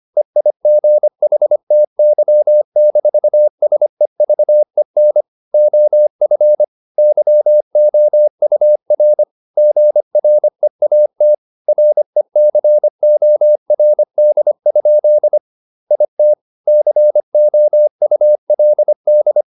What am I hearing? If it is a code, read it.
Morse code, heard as a keyed tone, about 25 wpm, 600 Hz. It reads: EIGHTY-SEVEN OF YOUR GREAT RECORD? IT COULD
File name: the old man and the sea part 1 25wpm